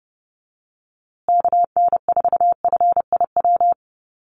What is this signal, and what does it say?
Morse code, heard as a keyed tone, about 30 wpm, 700 Hz. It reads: KN4FIW
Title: USA Call Signs Repeated 2x 30wpm